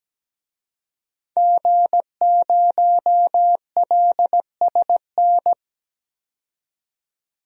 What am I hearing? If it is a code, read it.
Morse code, heard as a keyed tone, about 17 wpm, 700 Hz. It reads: G0LSN
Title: DXCC-100 Call Signs - 5 Characters - 17wpm